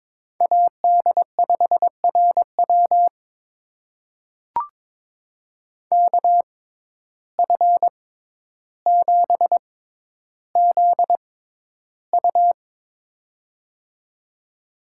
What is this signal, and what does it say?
Morse code, heard as a keyed tone, about 22 wpm, spaced as for 11 wpm, 700 Hz. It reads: AD5RW KF7ZU
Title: USA Call Signs - 5 Characters - 6x Character Spacing - 22wpm